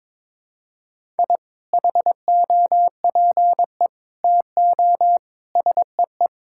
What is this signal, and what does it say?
Morse code, heard as a keyed tone, about 22 wpm, 700 Hz. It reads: I HOPE TO SEE
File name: practice / 22wpm / Sentences with 2-Word Phrases - 22wpm